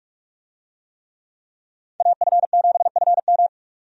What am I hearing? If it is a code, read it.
Morse code, heard as a keyed tone, about 46 wpm, 700 Hz. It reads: AF7FM
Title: USA Call Signs - 5 Characters - 45wpm